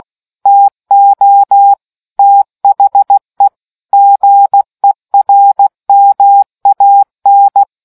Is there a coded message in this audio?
Yes: Morse code from 0.5 to 7.6 s, heard as a keyed tone, about 16 wpm, 800 Hz. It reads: TO THE GERMAN